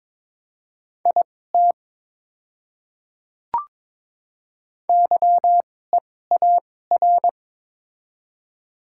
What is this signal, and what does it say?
Morse code, heard as a keyed tone, about 22 wpm, spaced as for 16 wpm, 700 Hz. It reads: IT YEAR
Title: 2x Character Spacing - Top 200 Words - 22wpm